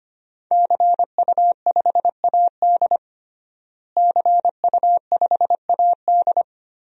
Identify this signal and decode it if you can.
Morse code, heard as a keyed tone, about 25 wpm, 700 Hz. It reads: CU5AD CU5AD
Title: DXCC-100 Call Signs - 5 Characters - Repeated 2x - 25wpm